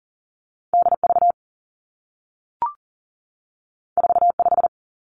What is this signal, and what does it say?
Morse code, heard as a keyed tone, about 40 wpm, 700 Hz. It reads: DV 45
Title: Two Challenging Letter-Number Combinations 40wpm